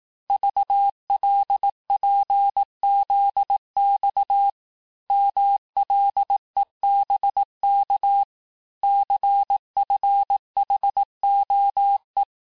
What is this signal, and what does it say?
Morse code, heard as a keyed tone, about 18 wpm, 800 Hz. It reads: VLPZX MLEBK CFHOE
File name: cours-du-lundi files / cours-1588-texte-chiffre-vitesse-17MPM